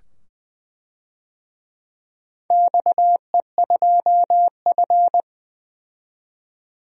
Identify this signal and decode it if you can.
Morse code, heard as a keyed tone, about 20 wpm, 700 Hz. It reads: XE2F
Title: DXCC-100 Call Signs - 4 Characters - 20wpm